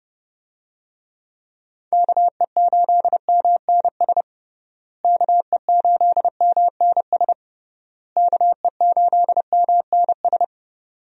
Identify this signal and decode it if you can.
Morse code, heard as a keyed tone, about 29 wpm, 700 Hz. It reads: KE8MNS KE8MNS KE8MNS
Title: USA Call Signs - 6 Characters - Repeated 3x - 30wpm